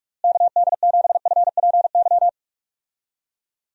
Morse code, heard as a keyed tone, about 46 wpm, 700 Hz. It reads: KD7FPY
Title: USA Call Signs - 6 Characters - Repeated 3x - 45wpm